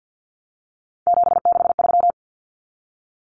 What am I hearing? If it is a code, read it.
Morse code, heard as a keyed tone, about 49 wpm, 700 Hz. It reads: 763
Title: Numbers - 3-Digit 50wpm